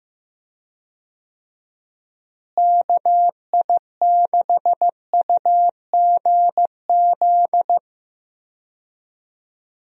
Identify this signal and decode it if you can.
Morse code, heard as a keyed tone, about 15 wpm, 700 Hz. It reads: KI6UGZ
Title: USA Call Signs - 6 Characters - Repeated 2x - 15wpm